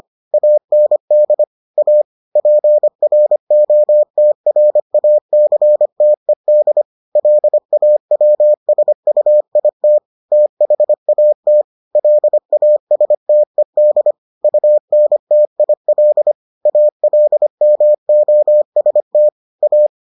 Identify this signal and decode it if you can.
Morse code, heard as a keyed tone, about 25 wpm, 600 Hz. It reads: AND A PROTRACTED LAWSUIT THAT LASTED UNTIL ALMOST A